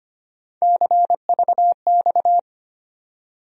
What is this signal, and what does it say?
Morse code, heard as a keyed tone, about 25 wpm, 700 Hz. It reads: CVX
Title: USA Call Sign Postfixes - 3 Letters 25wpm